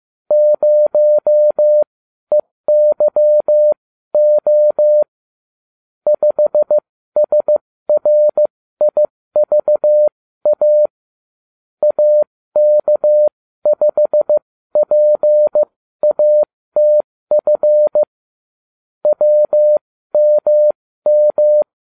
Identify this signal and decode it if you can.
Morse code, heard as a keyed tone, about 15 wpm, spaced as for 13 wpm, 600 Hz. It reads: 0EYO 5SRIVA AK5PATF WMM